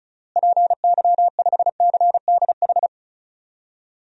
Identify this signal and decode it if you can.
Morse code, heard as a keyed tone, about 34 wpm, 700 Hz. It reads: PY5CDH
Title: DXCC-100 Call Signs - 6 Characters - Repeated 3x - 35wpm